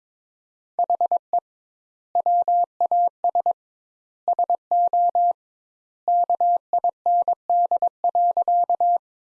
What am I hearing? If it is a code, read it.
Morse code, heard as a keyed tone, about 22 wpm, 700 Hz. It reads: HE WAS SO KIND.
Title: Sentences Top 200 Words - 22wpm